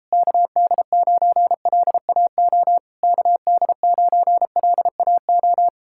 Morse code, heard as a keyed tone, about 37 wpm, 700 Hz. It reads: KD9LAO KD9LAO